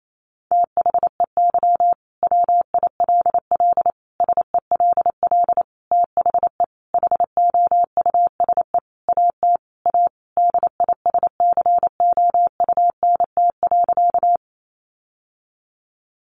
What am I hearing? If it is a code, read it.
Morse code, heard as a keyed tone, about 28 wpm, 700 Hz. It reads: THEY WILL SELL THE HOUSE AT A DISCOUNT.
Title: Sentences from Top 1200 Words 28wpm